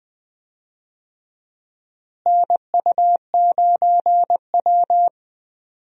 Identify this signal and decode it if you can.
Morse code, heard as a keyed tone, about 20 wpm, 690 Hz. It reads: NU9W
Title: USA Call Signs - 4 Characters - Repeated 3x - 20wpm